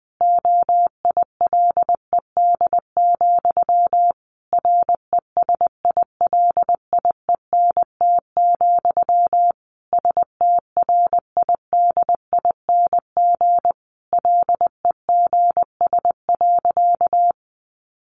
Morse code, heard as a keyed tone, about 20 wpm, 700 Hz. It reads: OILED, RESILIENT, STRIDING LEGS.
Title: A Sound Of Thunder 20wpm